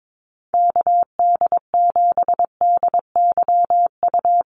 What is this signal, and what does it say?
Morse code, heard as a keyed tone, about 22 wpm, 700 Hz. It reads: KD7DYU
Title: USA Call Signs Repeated 3x 22wpm